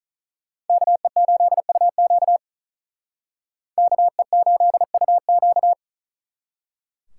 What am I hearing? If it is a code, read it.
Morse code, heard as a keyed tone, about 46 wpm, 700 Hz. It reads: KE8UQ KE8UQ